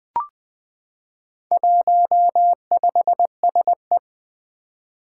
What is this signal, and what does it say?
Morse code, heard as a keyed tone, about 20 wpm, 700 Hz. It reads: 15SE
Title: Four Challenging Letter-Number Combinations 20wpm